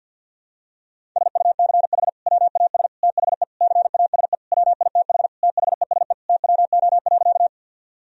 Morse code, heard as a keyed tone, about 52 wpm, 700 Hz. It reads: SUCH WAS THE CASE WITH THESE TWO.
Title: Sentences from Top 1000 Words - Review Part 1 - 50wpm